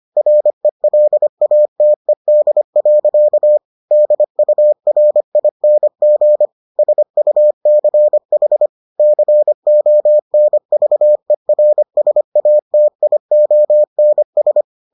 Morse code, heard as a keyed tone, about 25 wpm, 600 Hz. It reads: RELATED. DURING SUCH CONVERSATIONS